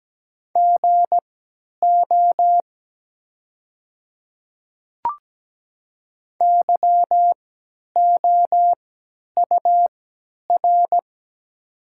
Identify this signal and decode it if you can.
Morse code, heard as a keyed tone, about 17 wpm, spaced as for 10 wpm, 700 Hz. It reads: GO YOUR
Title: 3x Character Spacing - Top 100 Words - 17wpm